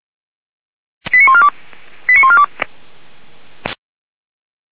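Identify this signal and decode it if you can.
Five-tone selective call (ZVEI), heard as quick runs of five tones, tones of about 70 ms. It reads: 89142 89142
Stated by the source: Five-tone ZVEI selective call received on an SDR, logged by the fire brigade of Orlamünde (feuerwehr-orlamuende).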